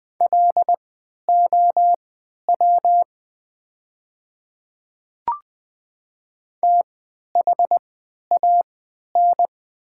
Morse code, heard as a keyed tone, about 20 wpm, spaced as for 11 wpm, 700 Hz. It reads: LOW THAN
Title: 3x Character Spacing - Top 300 Words - 20wpm